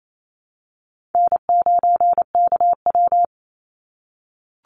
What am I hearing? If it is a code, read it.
Morse code, heard as a keyed tone, about 28 wpm, 700 Hz. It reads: N9KW
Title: NA SOTA Chaser Callsigns 28wpm